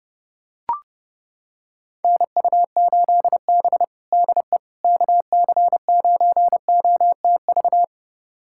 Morse code, heard as a keyed tone, about 30 wpm, 700 Hz. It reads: NU8B DE KC9OTV
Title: USA Callsign DE Callsign 30wpm